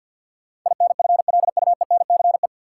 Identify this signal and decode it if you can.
Morse code, heard as a keyed tone, about 50 wpm, 700 Hz. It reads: INFLUENCE
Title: Sentences from Top 1100 Words - New Word Drill - 50wpm